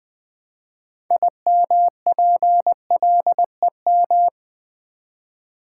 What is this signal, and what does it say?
Morse code, heard as a keyed tone, about 20 wpm, 700 Hz. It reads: IMPLEM
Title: Build A Word - 3 Letters At A Time - 9-Letter Words - 20wpm